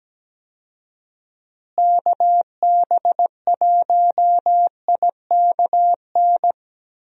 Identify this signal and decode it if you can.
Morse code, heard as a keyed tone, about 17 wpm, 700 Hz. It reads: KB1IKN
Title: USA Call Signs - 6 Characters - Repeated 2x - 17wpm